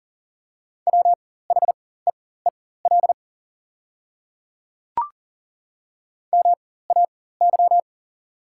Morse code, heard as a keyed tone, about 38 wpm, spaced as for 18 wpm, 690 Hz. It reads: WHEEL MAY
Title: 4x Character Spacing - Top 500 Words - 40wpm